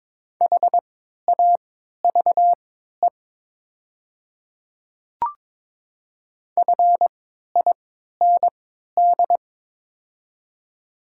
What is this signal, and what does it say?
Morse code, heard as a keyed tone, about 22 wpm, spaced as for 12 wpm, 700 Hz. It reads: HAVE FIND
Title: 3x Character Spacing - Top 100 Words - 22wpm